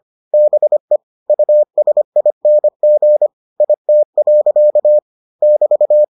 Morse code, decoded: BE USING IT. <BT>

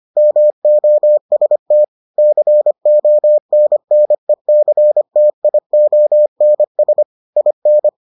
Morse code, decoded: MOST CONNECTIONS IN

